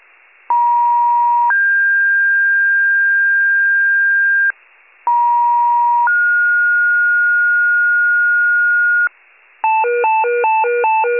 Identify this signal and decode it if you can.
Two-tone page, heard as a one-second tone, then a three-second tone, about 950 then 1600 Hz, about 950 then 1430 Hz, with noise. It is heard from 0.5 to 9.1 s.
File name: 83 RESCUE DUTY